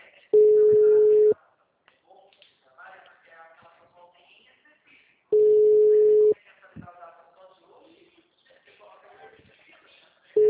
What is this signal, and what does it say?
Ringback tone (Europe, 425 Hz), with speech.